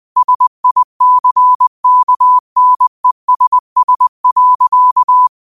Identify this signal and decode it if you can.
Morse code, heard as a keyed tone, about 20 wpm, 1000 Hz. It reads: SICKNESS.
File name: qotdfiles / 20wpm